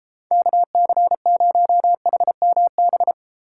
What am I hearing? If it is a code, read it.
Morse code, heard as a keyed tone, about 34 wpm, 700 Hz. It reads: KC0HMB
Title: Speed Racing - USA 2x3 Call Signs 28wpm